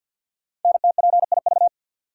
Morse code, heard as a keyed tone, about 47 wpm, 700 Hz. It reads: NTPIV
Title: Code Groups - 5 Letter:Numbers - 50wpm